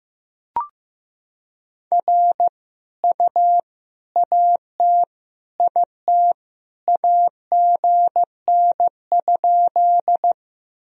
Morse code, heard as a keyed tone, about 16 wpm, 700 Hz. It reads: R U AT IT AGN?
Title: Sentences from Top 200 Words and Abbreviations 15wpm